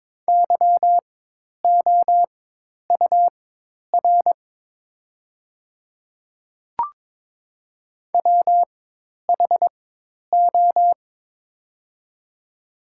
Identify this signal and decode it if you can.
Morse code, heard as a keyed tone, about 22 wpm, spaced as for 10 wpm, 690 Hz. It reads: YOUR WHO
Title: 4x Character Spacing - Top 100 Words - 22wpm